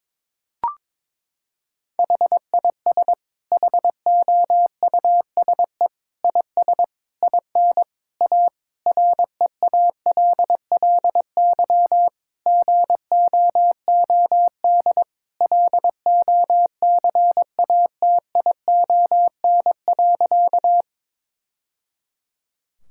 Morse code, heard as a keyed tone, about 22 wpm, 700 Hz. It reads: HIS HOUSE IS IN A REALLY GOOD LOCATION.
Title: Sentences from Top 1000 Words - Review Part 6 - 22wpm